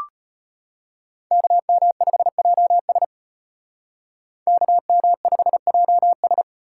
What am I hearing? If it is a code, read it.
Morse code, heard as a keyed tone, about 32 wpm, 700 Hz. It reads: KM5JS KM5JS